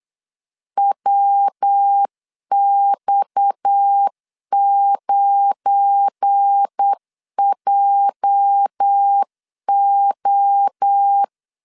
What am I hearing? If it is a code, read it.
Morse code, heard as a keyed tone, about 8 wpm, 800 Hz. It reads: WX9JO
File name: LEZIONI / 9 28 8wpm 3512